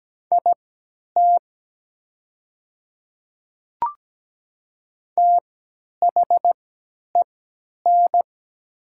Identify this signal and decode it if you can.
Morse code, heard as a keyed tone, about 17 wpm, spaced as for 10 wpm, 700 Hz. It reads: IT THEN
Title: 3x Character Spacing - Top 100 Words - 17wpm